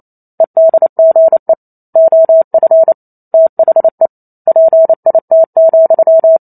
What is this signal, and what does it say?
Morse code, heard as a keyed tone, about 28 wpm, 650 Hz. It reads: EDGE OF THE PIT,
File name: Chapter 2 - 28 WPM